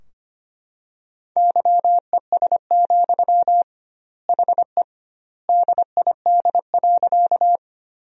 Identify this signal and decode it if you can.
Morse code, heard as a keyed tone, about 25 wpm, 700 Hz. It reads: YES, HE DID.
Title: 2x Word-Spacing - Sentences from Top 500 Words - 25wpm